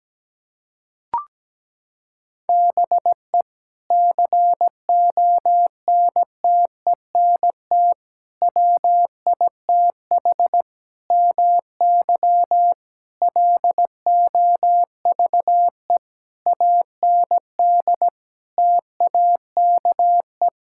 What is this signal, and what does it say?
Morse code, heard as a keyed tone, about 17 wpm, 700 Hz. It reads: BE CONTENT WITH MY LOVE AND TAKE
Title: Sentences from Top 1000 Words - Review Part 6 - 17wpm